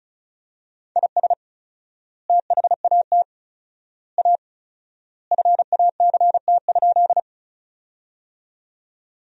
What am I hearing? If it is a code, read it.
Morse code, heard as a keyed tone, about 36 wpm, 700 Hz. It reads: IS THAT A FACT?
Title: 4x Word-Spacing - Sentences from Top 500 Words - 35wpm